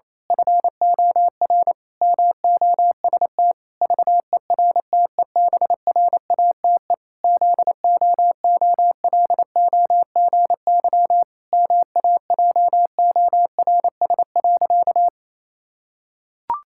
Morse code, heard as a keyed tone, about 28 wpm, 700 Hz. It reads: FOR MOST VERTEBRATE ZOOLOGY MAJORS.
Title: Pangram Challenge v2 - 28wpm